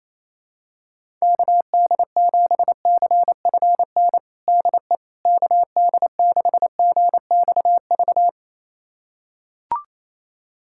Morse code, heard as a keyed tone, about 28 wpm, 700 Hz. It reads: KD7CFN DE KD6GXV
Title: USA Callsign DE Callsign 28wpm